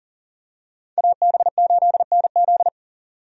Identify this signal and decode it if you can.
Morse code, heard as a keyed tone, about 40 wpm, 700 Hz. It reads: AB8NZ
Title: USA Call Signs - 5 Characters - 40wpm